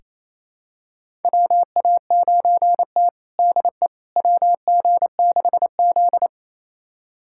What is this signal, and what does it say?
Morse code, heard as a keyed tone, about 28 wpm, 700 Hz. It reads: WA9T DE WG6Z
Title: USA Callsign DE Callsign - 4 Characters - 28wpm